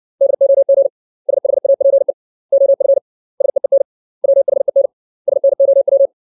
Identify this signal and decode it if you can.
Morse code, heard as a keyed tone, about 60 wpm, 550 Hz. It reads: DOG SHAPE OF SEEN WHEN SNOW
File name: 632 words / 632 Words 60WPM